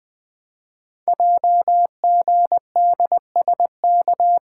Morse code, heard as a keyed tone, about 20 wpm, 700 Hz. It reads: JGDSK
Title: Code Groups - 5 Letter:Numbers - 20wpm